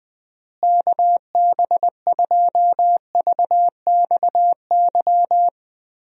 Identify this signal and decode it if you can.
Morse code, heard as a keyed tone, about 20 wpm, 700 Hz. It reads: KB2VXY
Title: USA Call Signs - 6 Characters - Repeated 3x - 20wpm